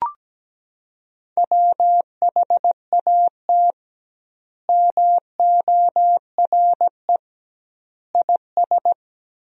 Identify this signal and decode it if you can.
Morse code, heard as a keyed tone, about 17 wpm, 700 Hz. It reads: WHAT MORE IS
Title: 2x Word-Spacing - Sentences from Top 500 Words - 17wpm